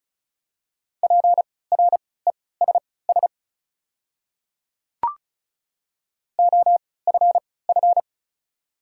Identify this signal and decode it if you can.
Morse code, heard as a keyed tone, about 34 wpm, spaced as for 20 wpm, 700 Hz. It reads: PRESS OFF
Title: 3x Character Spacing - Top 300 Words - 35wpm